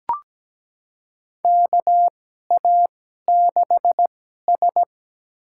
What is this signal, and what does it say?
Morse code, heard as a keyed tone, about 17 wpm, spaced as for 12 wpm, 700 Hz. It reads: KA6S
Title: USA Call Signs - 2x Character Spacing - 17wpm